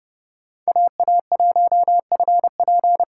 Morse code, heard as a keyed tone, about 29 wpm, 700 Hz. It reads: AA1FP
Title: USA Call Signs - 2x Character Spacing - 30wpm